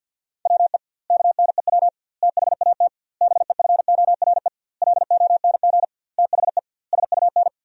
Morse code, heard as a keyed tone, about 50 wpm, 700 Hz. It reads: WE KNEW THAT BEFORE LONG THE SUN
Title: 2x Word-Spacing - Sentences from Top 400 Words - 50wpm